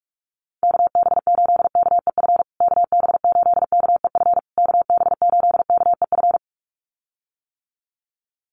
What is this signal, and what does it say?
Morse code, heard as a keyed tone, about 44 wpm, 700 Hz. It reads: KB8KEF KB8KEF KB8KEF
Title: USA Call Signs Repeated 3x 45wpm